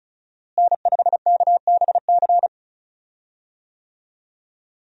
Morse code, heard as a keyed tone, about 36 wpm, 700 Hz. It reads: N5KBC
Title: USA Call Signs - 5 Characters - Repeated 3x - 35wpm